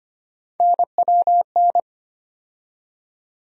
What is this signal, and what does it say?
Morse code, heard as a keyed tone, about 25 wpm, 700 Hz. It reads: NWN
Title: USA Call Sign Postfixes - 3 Letters 25wpm